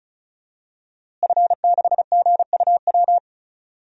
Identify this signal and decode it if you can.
Morse code, heard as a keyed tone, about 34 wpm, 700 Hz. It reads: F6GUW
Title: DXCC-100 Call Signs - 5 Characters - Repeated 2x - 35wpm